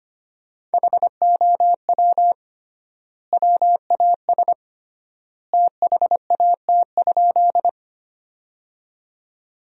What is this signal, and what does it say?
Morse code, heard as a keyed tone, about 25 wpm, 700 Hz. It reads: HOW WAS THAT?
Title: 3x Word-Spacing - Sentences from Top 100 Words - 25wpm